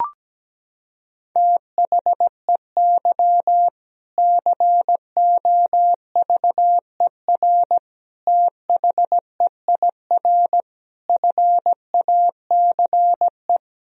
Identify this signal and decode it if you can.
Morse code, heard as a keyed tone, about 17 wpm, 700 Hz. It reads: THEY COVER THEIR FACE
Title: Sentences from Top 1000 Words - Review Part 1 - 17wpm